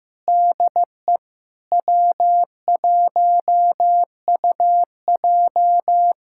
Morse code, heard as a keyed tone, about 16 wpm, 700 Hz. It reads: DE W1UJ
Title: K1USN SST QSO Practice 15wpm